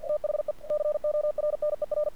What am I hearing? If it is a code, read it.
Morse code, heard as a keyed tone, about 48 wpm, 600 Hz. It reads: THE CORNER
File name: archived / CRSnet